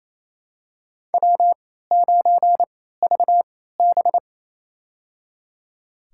Morse code, heard as a keyed tone, about 28 wpm, spaced as for 16 wpm, 700 Hz. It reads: W9VB
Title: USA Call Signs - 4 Characters - 3x Character Spacing - 28wpm